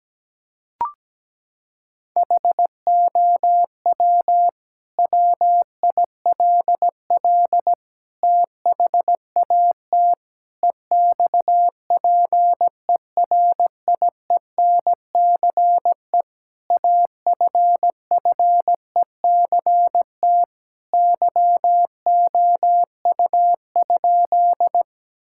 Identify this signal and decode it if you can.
Morse code, heard as a keyed tone, about 17 wpm, 700 Hz. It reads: HOW WILL THAT EXPERIENCE AFFECT YOU?
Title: Sentences from Top 1000 Words - Review Part 6 - 17wpm